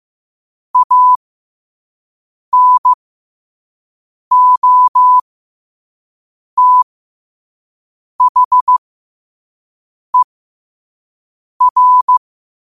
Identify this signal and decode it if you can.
Morse code, heard as a keyed tone, about 15 wpm, spaced as for 5 wpm, 1000 Hz. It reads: ANOTHER
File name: qotdfiles / 5wpm